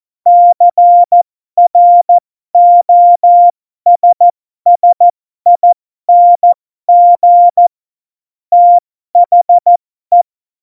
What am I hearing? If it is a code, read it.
Morse code, heard as a keyed tone, about 14 wpm, spaced as for 12 wpm, 700 Hz. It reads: CROSSING THE
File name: Morsekurs / Witze